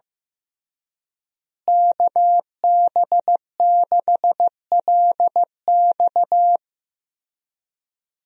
Morse code, heard as a keyed tone, about 15 wpm, 700 Hz. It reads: KB6LX
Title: USA Call Signs - 5 Characters - Repeated 2x - 15wpm